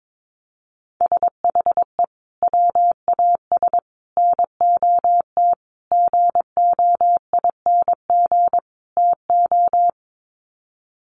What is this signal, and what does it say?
Morse code, heard as a keyed tone, about 22 wpm, 700 Hz. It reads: SHE WAS NOT GOING TO
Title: Sets of 5 Words Encore2 22wpm